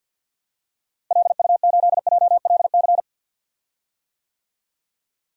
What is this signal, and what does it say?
Morse code, heard as a keyed tone, about 47 wpm, 700 Hz. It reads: LU8JLC
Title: DXCC-100 Call Signs - 6 Characters - Repeated 2x - 50wpm